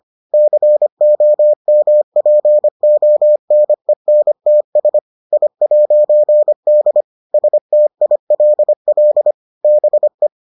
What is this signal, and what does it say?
Morse code, heard as a keyed tone, about 25 wpm, 600 Hz. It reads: COMPONENTS I'D STILL BE